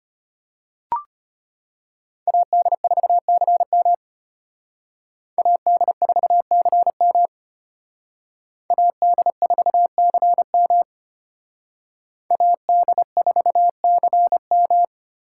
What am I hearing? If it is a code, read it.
Morse code, heard as a keyed tone, about 32 wpm, 700 Hz. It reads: AD4CM AD4CM AD4CM AD4CM